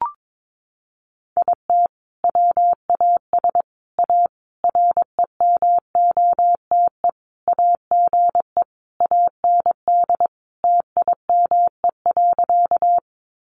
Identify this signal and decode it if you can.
Morse code, heard as a keyed tone, about 22 wpm, 700 Hz. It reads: IT WAS A REMOTE AGE AND TIME.
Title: Sentences from Top 1300 Words 22wpm